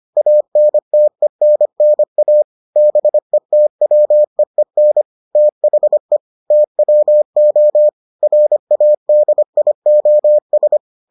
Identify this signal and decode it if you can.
Morse code, heard as a keyed tone, about 25 wpm, 600 Hz. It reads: ANTENNA BETWEEN THE TWO RADIOS